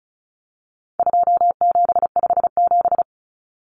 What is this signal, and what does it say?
Morse code, heard as a keyed tone, about 34 wpm, 700 Hz. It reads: 2757
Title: Common Cut Numbers - 4-Digit 35wpm